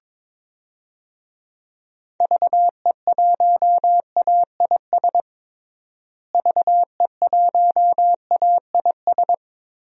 Morse code, heard as a keyed tone, about 21 wpm, 700 Hz. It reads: VE1AIS VE1AIS
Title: DXCC-100 Call Signs - 6 Characters - Repeated 2x - 22wpm